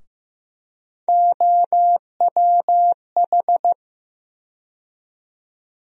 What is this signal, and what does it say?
Morse code, heard as a keyed tone, about 15 wpm, 700 Hz. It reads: OWH
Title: USA Call Sign Postfixes - 3 Letters 15wpm